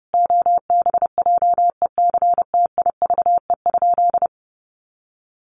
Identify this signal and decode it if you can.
Morse code, heard as a keyed tone, about 30 wpm, 700 Hz. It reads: OBJECTIVE?
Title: Sentences from Top 1200 Words 30wpm